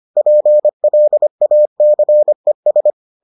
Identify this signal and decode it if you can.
Morse code, heard as a keyed tone, about 25 wpm, 600 Hz. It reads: PLACES